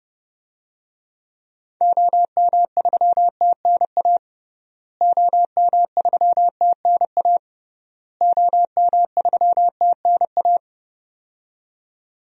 Morse code, heard as a keyed tone, about 30 wpm, 700 Hz. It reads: OM3TNA OM3TNA OM3TNA
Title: DXCC-100 Call Signs - 6 Characters - Repeated 3x - 30wpm